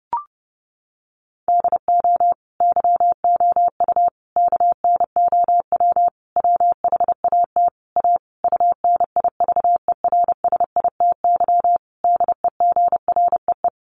Morse code, heard as a keyed tone, about 30 wpm, 700 Hz. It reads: DO YOU KNOW WHAT A UNIVERSITY DEGREE
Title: Sentences from Top 1500 Words 30wpm